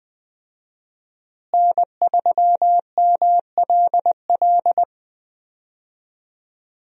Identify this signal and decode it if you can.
Morse code, heard as a keyed tone, about 21 wpm, 700 Hz. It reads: N3MLL
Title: USA Call Signs - 5 Characters - Repeated 3x - 20wpm